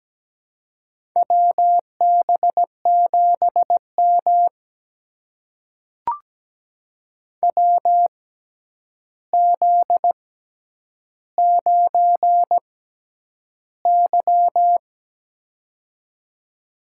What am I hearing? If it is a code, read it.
Morse code, heard as a keyed tone, about 17 wpm, spaced as for 9 wpm, 700 Hz. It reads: WB7M WZ9Y
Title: USA Call Signs - 4 Characters - 6x Character Spacing - 17wpm